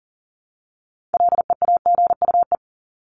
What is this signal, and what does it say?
Morse code, heard as a keyed tone, about 40 wpm, 700 Hz. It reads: LEAGUE
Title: Top 1751-2000 Words 40wpm